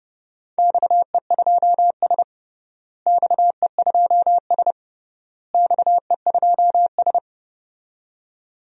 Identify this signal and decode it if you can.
Morse code, heard as a keyed tone, about 30 wpm, 700 Hz. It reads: XE2S XE2S XE2S
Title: DXCC-100 Call Signs - 4 Characters - Repeated 3x - 30wpm